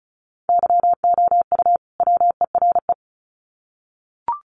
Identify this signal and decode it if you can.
Morse code, heard as a keyed tone, about 34 wpm, 700 Hz. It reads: YOU WERE
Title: Sets of 2 Words 35wpm